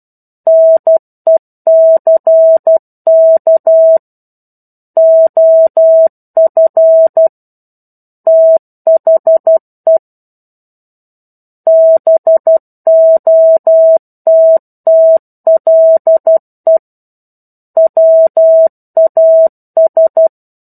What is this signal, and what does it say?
Morse code, heard as a keyed tone, about 12 wpm, 650 Hz. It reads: NECK OF THE BOTTLE WAS
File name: 01-Chapter-01 12 WPM